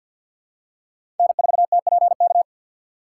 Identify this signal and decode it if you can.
Morse code, heard as a keyed tone, about 52 wpm, 700 Hz. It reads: N4TPK
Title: USA Call Signs - 5 Characters - 6x Character Spacing - 50wpm